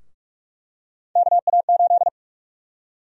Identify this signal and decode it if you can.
Morse code, heard as a keyed tone, about 46 wpm, 700 Hz. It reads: KA8
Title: USA Call Sign Prefixes - 45wpm